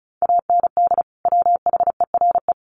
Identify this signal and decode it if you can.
Morse code, heard as a keyed tone, about 34 wpm, 700 Hz. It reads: AND WHERE
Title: Sentences from Top 1100 Words 35wpm